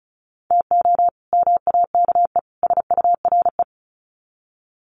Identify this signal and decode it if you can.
Morse code, heard as a keyed tone, about 34 wpm, 700 Hz. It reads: TO MAKE SURE
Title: Sets of 3 Words 35wpm